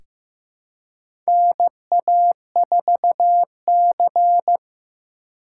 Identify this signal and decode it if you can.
Morse code, heard as a keyed tone, about 15 wpm, 700 Hz. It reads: NA4C
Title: K1USN SST QSO Practice - 15wpm